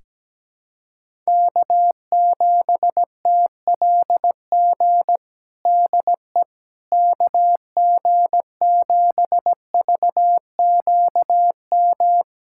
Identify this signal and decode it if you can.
Morse code, heard as a keyed tone, about 17 wpm, 700 Hz. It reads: K7TLG DE KG7VQM
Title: USA Callsign DE Callsign 17wpm